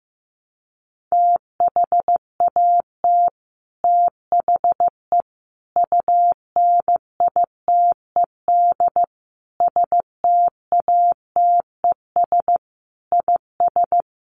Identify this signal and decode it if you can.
Morse code, heard as a keyed tone, about 15 wpm, 700 Hz. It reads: THAT THE UNITED STATES IS
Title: Sets of 5 Words Encore 15wpm